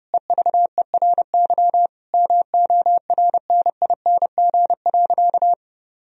Morse code, heard as a keyed tone, about 30 wpm, 700 Hz. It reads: EVERY MORNING.
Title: Sentences from Top 1000 Words - Review Part 3 - 30wpm